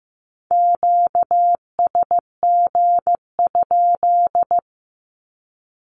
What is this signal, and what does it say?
Morse code, heard as a keyed tone, about 15 wpm, 700 Hz. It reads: QSG?
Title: Q-Codes 15wpm